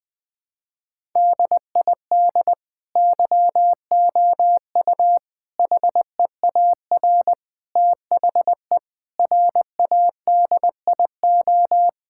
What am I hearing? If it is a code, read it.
Morse code, heard as a keyed tone, about 20 wpm, 700 Hz. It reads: DID YOU HEAR THE RADIO
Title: Sentences from Top 1000 Words - Review Part 5 - 20wpm